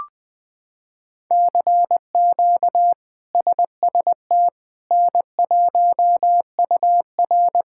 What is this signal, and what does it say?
Morse code, heard as a keyed tone, about 20 wpm, 700 Hz. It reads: CQ SST N1UR
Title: K1USN SST QSO Practice - 20wpm